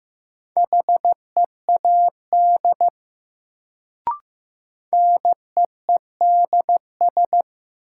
Morse code, heard as a keyed tone, about 15 wpm, 700 Hz. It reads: HEAD NEEDS
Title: Sets of 3 Words - New Word Drill - 15wpm